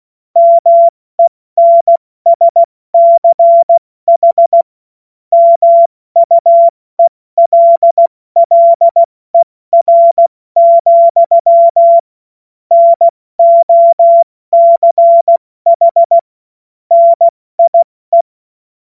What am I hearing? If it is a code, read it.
Morse code, heard as a keyed tone, about 16 wpm, 680 Hz. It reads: MENSCH MUELLER, NOCH NIE
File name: Morsekurs / Witze / Witz 17 Juni Mai 2025 70 F80